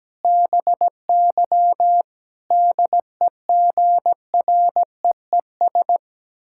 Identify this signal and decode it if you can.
Morse code, heard as a keyed tone, about 17 wpm, 700 Hz. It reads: BY DEGREES
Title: Pangram Challenge v2 - 17wpm